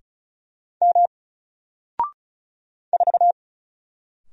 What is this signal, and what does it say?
Morse code, heard as a keyed tone, about 34 wpm, 700 Hz. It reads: M 4
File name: Single-Letter-Number-35wpm